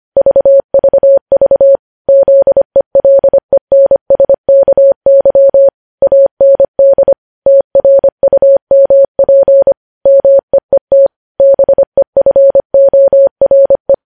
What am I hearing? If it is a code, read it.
Morse code, heard as a keyed tone, about 25 wpm, 550 Hz. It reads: VVV ZELENSKY AND TRUMP MEET BEFORE
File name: podcast / News headlines-1x-25x25